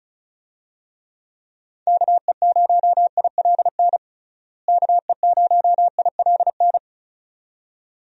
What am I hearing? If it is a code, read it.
Morse code, heard as a keyed tone, about 36 wpm, 700 Hz. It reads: KE0ILN KE0ILN
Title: USA Call Signs - 6 Characters - Repeated 2x - 35wpm